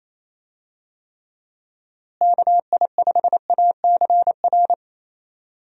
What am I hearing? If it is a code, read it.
Morse code, heard as a keyed tone, about 28 wpm, 700 Hz. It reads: KI5ACR
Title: USA Call Signs - 6 Characters - Repeated 2x - 28wpm